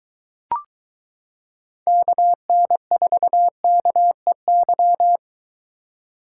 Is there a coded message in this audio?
Yes: Morse code from 1.9 to 5.2 s, heard as a keyed tone, about 22 wpm, 700 Hz. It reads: KN4KEY